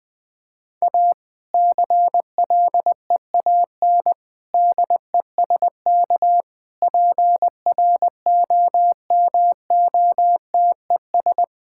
Morse code, heard as a keyed tone, about 20 wpm, 700 Hz. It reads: A CLEAN DESK PROMOTES
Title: Difficult Sentences with Hints - 20wpm